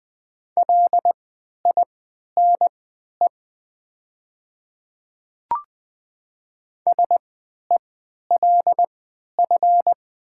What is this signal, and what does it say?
Morse code, heard as a keyed tone, about 20 wpm, spaced as for 11 wpm, 700 Hz. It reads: LINE SELF
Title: 3x Character Spacing - Top 200 Words - 20wpm